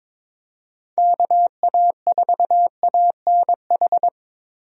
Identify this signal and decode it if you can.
Morse code, heard as a keyed tone, about 22 wpm, 700 Hz. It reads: KA4ANH
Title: USA Call Signs - 6 Characters - Repeated 2x - 22wpm